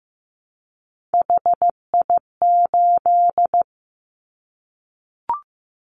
Morse code, heard as a keyed tone, about 15 wpm, 700 Hz. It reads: HI8
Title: Callsign Prefixes for Top 100 DXCC - 15wpm